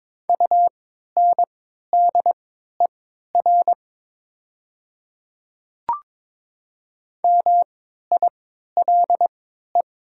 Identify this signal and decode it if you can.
Morse code, heard as a keyed tone, about 22 wpm, spaced as for 13 wpm, 700 Hz. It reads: UNDER MILE
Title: 3x Character Spacing - Top 300 Words - 22wpm